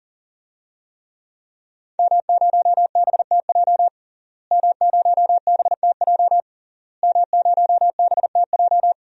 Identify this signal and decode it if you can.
Morse code, heard as a keyed tone, about 40 wpm, 700 Hz. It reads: M0BTJ M0BTJ M0BTJ
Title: DXCC-100 Call Signs - 5 Characters - Repeated 3x - 40wpm